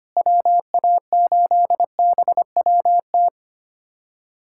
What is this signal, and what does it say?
Morse code, heard as a keyed tone, about 25 wpm, 700 Hz. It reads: WA8BWT
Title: USA Callsign DE Callsign 25wpm